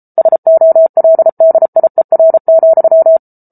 Morse code, heard as a keyed tone, about 34 wpm, 650 Hz. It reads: SOLDIER,